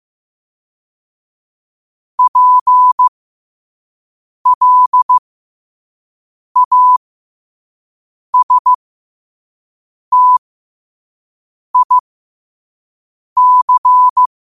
Morse code, heard as a keyed tone, about 15 wpm, spaced as for 5 wpm, 1000 Hz. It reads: PLASTIC
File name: qotdfiles / 5wpm